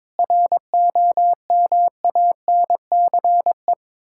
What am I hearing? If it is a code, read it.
Morse code, heard as a keyed tone, about 22 wpm, 690 Hz. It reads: ROMANCE
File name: Difficult Words with Hints - 22wpm